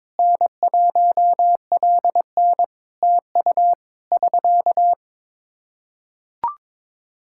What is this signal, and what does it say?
Morse code, heard as a keyed tone, about 22 wpm, 700 Hz. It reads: N1LN TU <SK>
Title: Practice QSO - Exchange 4 - Ending - 22wpm